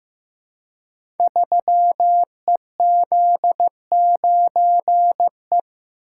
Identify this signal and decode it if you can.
Morse code, heard as a keyed tone, about 15 wpm, 700 Hz. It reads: 3EZ9E
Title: Code Groups - 5 Letter:Numbers - 15wpm